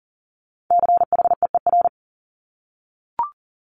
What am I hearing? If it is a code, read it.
Morse code, heard as a keyed tone, about 41 wpm, 700 Hz. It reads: CHEER
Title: Multi-letter Phonemes Repeated 3x 40wpm